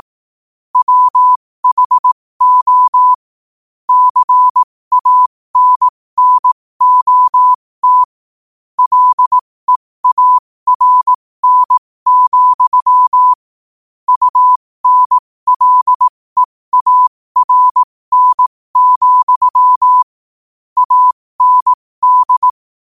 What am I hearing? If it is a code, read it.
Morse code, heard as a keyed tone, about 18 wpm, spaced as for 15 wpm, 1000 Hz. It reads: WHO CANNOT LEARN, UNLEARN, AND
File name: qotdfiles / 15wpm